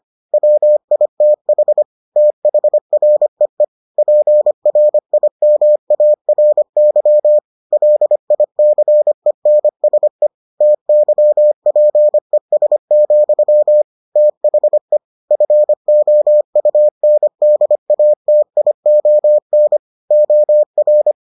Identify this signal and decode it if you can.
Morse code, heard as a keyed tone, about 25 wpm, 600 Hz. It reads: WITH THREE PRIMARY LICENSE TYPES, THE FOUNDATION OR